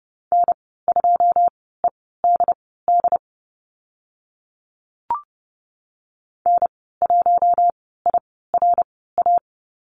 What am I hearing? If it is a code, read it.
Morse code, heard as a keyed tone, about 30 wpm, spaced as for 17 wpm, 700 Hz. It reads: N2EDD N1IRA
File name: Lesson 15g - Number 5 - Call Signs 30wpm